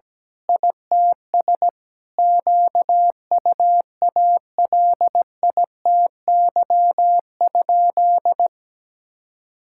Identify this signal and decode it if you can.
Morse code, heard as a keyed tone, about 17 wpm, 700 Hz. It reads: ITS QUALITY?
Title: Sentences from Top 1000 Words - Review Part 6 - 17wpm